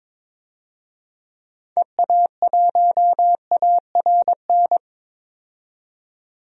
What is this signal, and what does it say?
Morse code, heard as a keyed tone, about 22 wpm, 700 Hz. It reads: EA1ARN
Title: DXCC-100 Call Signs - 6 Characters - 22wpm